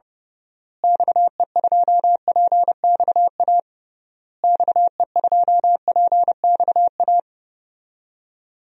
Morse code, heard as a keyed tone, about 30 wpm, 700 Hz. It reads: XE2PXA XE2PXA
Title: DXCC-100 Call Signs - 6 Characters - Repeated 3x - 30wpm